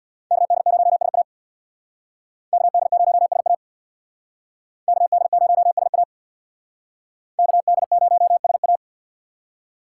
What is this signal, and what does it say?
Morse code, heard as a keyed tone, about 63 wpm, 700 Hz. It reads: KD0SA KD0SA KD0SA KD0SA